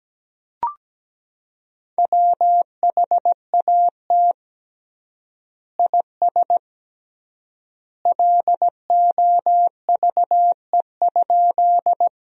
Morse code, heard as a keyed tone, about 17 wpm, 700 Hz. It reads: WHAT IS LOVE?
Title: 3x Word-Spacing - Sentences from Top 400 Words - 17wpm